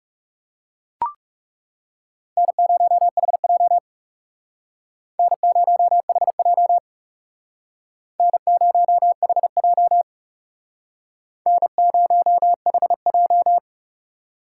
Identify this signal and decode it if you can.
Morse code, heard as a keyed tone, about 38 wpm, 700 Hz. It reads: N0HJ N0HJ N0HJ N0HJ
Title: Speed Racing - USA 1x2 Call Signs 30wpm